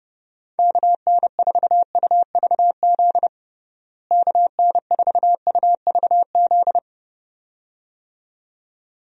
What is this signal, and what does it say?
Morse code, heard as a keyed tone, about 30 wpm, 700 Hz. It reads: KN4UVZ KN4UVZ
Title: USA Call Signs - 6 Characters - Repeated 2x - 30wpm